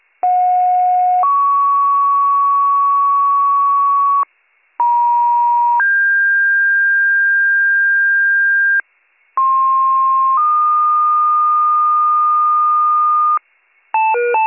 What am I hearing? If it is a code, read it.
Two-tone page, heard as a one-second tone, then a three-second tone, about 730 then 1120 Hz, about 950 then 1600 Hz, about 1060 then 1220 Hz. It is heard from 0.2 to 13.4 s.